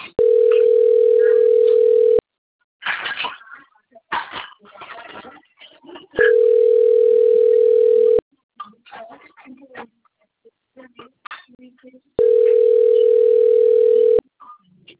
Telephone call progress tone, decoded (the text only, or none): ringback tone